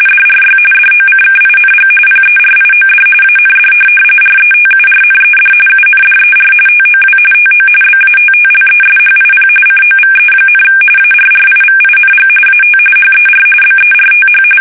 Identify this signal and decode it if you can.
RTTY, about 45.45 baud. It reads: DRY DOCKING PERIOD AND RETURNED UNDER TOW TO BATTLESHIP COVE, ARRIVING AT HER BERTH
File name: fsk850 / USS-Massachusetts